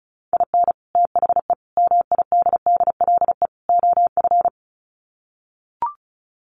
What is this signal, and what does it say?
Morse code, heard as a keyed tone, about 34 wpm, 700 Hz. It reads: IN THE MIDDLE OF
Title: Sets of 4 Words 35wpm